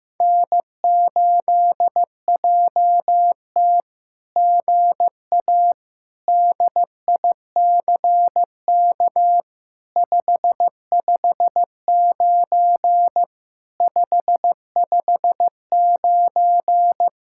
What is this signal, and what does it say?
Morse code, heard as a keyed tone, about 15 wpm, 700 Hz. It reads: N8JT GA DICK 559 559
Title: SOTA QSO Practice 15wpm